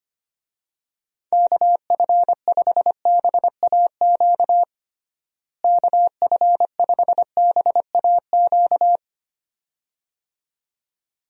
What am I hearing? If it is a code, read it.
Morse code, heard as a keyed tone, about 25 wpm, 700 Hz. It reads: KF5BAQ KF5BAQ
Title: USA Call Signs - 6 Characters - Repeated 2x - 25wpm